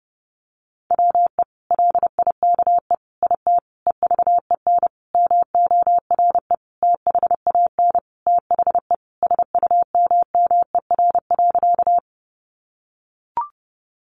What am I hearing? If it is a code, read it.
Morse code, heard as a keyed tone, about 28 wpm, 700 Hz. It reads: WE LIKE IT EVEN MORE THAN THE SUMMER.
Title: Sentences from Top 1100 Words 30wpm